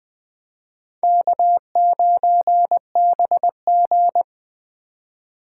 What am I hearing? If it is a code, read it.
Morse code, heard as a keyed tone, about 20 wpm, 700 Hz. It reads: K9BG
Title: Speed Racing - USA Call Signs 17wpm